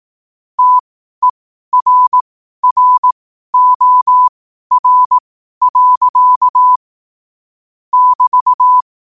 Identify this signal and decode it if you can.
Morse code, heard as a keyed tone, about 18 wpm, spaced as for 12 wpm, 1000 Hz. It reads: TERROR. <BT>
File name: qotdfiles / 12wpm